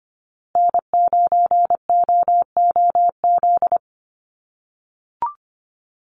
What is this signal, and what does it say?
Morse code, heard as a keyed tone, about 24 wpm, 700 Hz. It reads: N9OOZ
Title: USA Call Signs Repeated 3x 25wpm